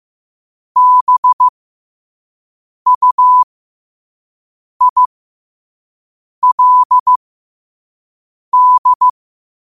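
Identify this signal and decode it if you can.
Morse code, heard as a keyed tone, about 15 wpm, spaced as for 5 wpm, 1000 Hz. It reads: BUILD
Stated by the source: Morse at 5 wpm.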